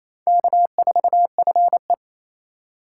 Morse code, heard as a keyed tone, about 35 wpm, 700 Hz. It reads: K4FE
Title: Speed Racing - USA Call Signs 28wpm